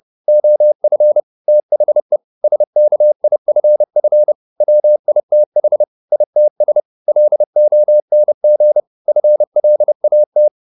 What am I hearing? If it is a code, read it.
Morse code, heard as a keyed tone, about 30 wpm, 600 Hz. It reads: OF THE SKIFF WITH ITS LONG FLAT